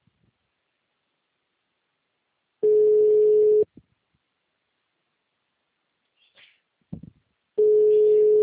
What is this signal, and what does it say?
Ringback tone (Europe, 425 Hz).